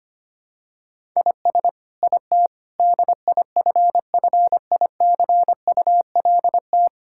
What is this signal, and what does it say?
Morse code, heard as a keyed tone, about 25 wpm, 700 Hz. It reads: IS IT DIFFICULT
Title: Sentences from Top 1000 Words - Review Part 5 - 25wpm